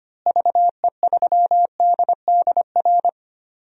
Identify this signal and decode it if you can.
Morse code, heard as a keyed tone, about 25 wpm, 700 Hz. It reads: VE3DDR
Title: DXCC-100 Call Signs - 6 Characters - Repeated 3x - 25wpm